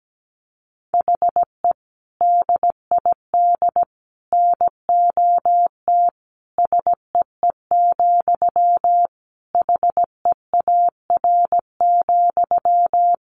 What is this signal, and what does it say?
Morse code, heard as a keyed tone, about 17 wpm, 700 Hz. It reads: HE DID NOT SEE, HEAR,
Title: Sentences from Top 1300 Words 17wpm